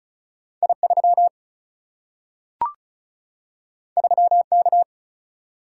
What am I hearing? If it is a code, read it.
Morse code, heard as a keyed tone, about 36 wpm, 700 Hz. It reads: I 3 3 K
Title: Single-Letter Single-Number 35wpm